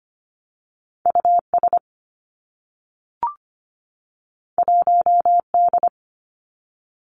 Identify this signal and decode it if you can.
Morse code, heard as a keyed tone, about 25 wpm, 700 Hz. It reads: US 1D
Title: Two Challenging Letter-Number Combinations 25wpm